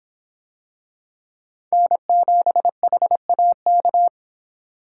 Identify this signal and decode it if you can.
Morse code, heard as a keyed tone, about 26 wpm, 700 Hz. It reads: N7HAK